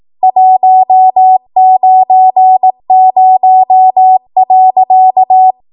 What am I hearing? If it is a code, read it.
Morse code, heard as a keyed tone, about 18 wpm, 750 Hz. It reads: 190.
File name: Morse / 18WPM